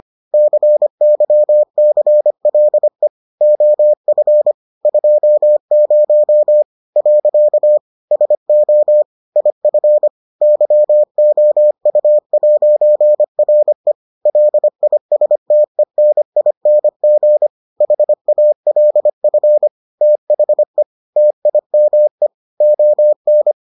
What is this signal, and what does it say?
Morse code, heard as a keyed tone, about 25 wpm, 600 Hz. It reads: CYCLE OF 20 . SO IF YOU'RE LISTENING HALF THE TIME ON